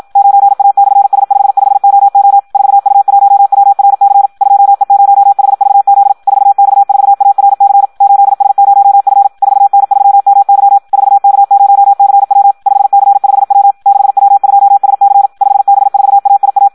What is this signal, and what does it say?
Morse code, heard as a keyed tone, about 56 wpm, 800 Hz. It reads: 9A/S55OO 4U1WRC 2E0HVZ 4X4ARC 8S0F 4N3MY 4K0CW 5C5W 6W2SC 5B4AIA
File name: Audio practica CW / qrq-tx-54 2